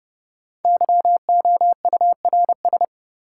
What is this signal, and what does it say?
Morse code, heard as a keyed tone, about 32 wpm, 700 Hz. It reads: YOURS
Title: Sentences from Top 1500 Words - New Word Drill Alt - 30wpm